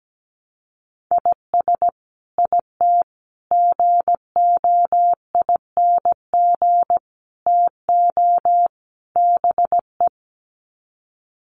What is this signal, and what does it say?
Morse code, heard as a keyed tone, about 17 wpm, 700 Hz. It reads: IS IT GOING TO BE
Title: Sets of 5 Words Encore2 17wpm